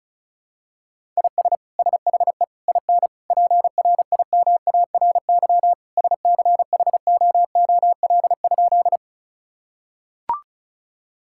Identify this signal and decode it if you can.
Morse code, heard as a keyed tone, about 36 wpm, 700 Hz. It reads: IS SHE IN PRIMARY SCHOOL?
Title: Sentences from Top 1000 Words - Review Part 5 - 35wpm